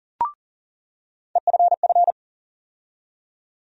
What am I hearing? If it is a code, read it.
Morse code, heard as a keyed tone, about 40 wpm, 700 Hz. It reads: EFF
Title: USA Call Sign Postfixes - 3 Letters 40wpm